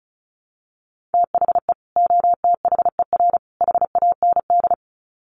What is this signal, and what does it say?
Morse code, heard as a keyed tone, about 34 wpm, 700 Hz. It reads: THE OTHER HAND
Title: Sets of 3 Words 35wpm